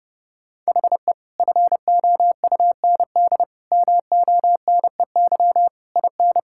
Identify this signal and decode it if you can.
Morse code, heard as a keyed tone, about 28 wpm, 700 Hz. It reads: HE FOUND MONEY IN
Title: Sentences from Top 1000 Words - Review Part 6 - 30wpm